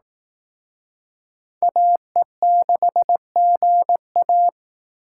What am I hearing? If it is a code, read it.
Morse code, heard as a keyed tone, about 18 wpm, 700 Hz. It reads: AE6GA
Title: Speed Racing - USA 2x2 Call Signs 15wpm